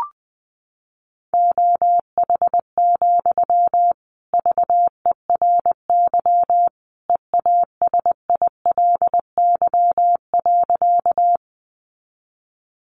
Morse code, heard as a keyed tone, about 20 wpm, 700 Hz. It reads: OH, VERY EASILY.
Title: Sentences from Top 900 Words 20wpm